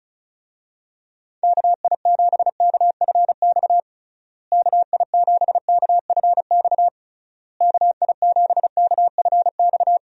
Morse code, heard as a keyed tone, about 34 wpm, 700 Hz. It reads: KI7KFX KI7KFX KI7KFX
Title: USA Call Signs - 6 Characters - Repeated 3x - 35wpm